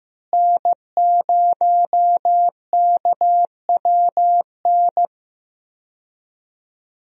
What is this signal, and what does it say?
Morse code, heard as a keyed tone, about 15 wpm, 700 Hz. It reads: N0KWN
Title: USA Call Signs - 5 Characters - Repeated 2x - 15wpm